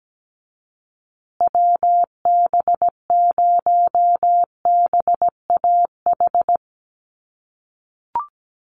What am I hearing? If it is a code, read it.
Morse code, heard as a keyed tone, about 17 wpm, 700 Hz. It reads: WB0BAH
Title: USA Call Signs Repeated 3x 17wpm